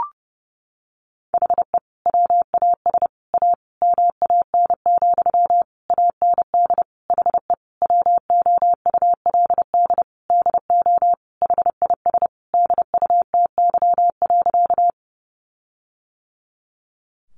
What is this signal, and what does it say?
Morse code, heard as a keyed tone, about 30 wpm, 700 Hz. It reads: HE WAS A MAN, AND HE WOULD DO HIS DUTY.
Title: Sentences from Top 1500 Words 30wpm